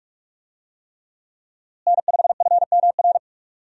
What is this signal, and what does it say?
Morse code, heard as a keyed tone, about 42 wpm, 700 Hz. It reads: N5FMR
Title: USA Call Signs - 5 Characters - Repeated 3x - 45wpm